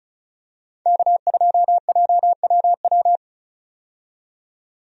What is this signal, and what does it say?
Morse code, heard as a keyed tone, about 36 wpm, 700 Hz. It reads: K2JWW
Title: USA Call Signs - 5 Characters - 35wpm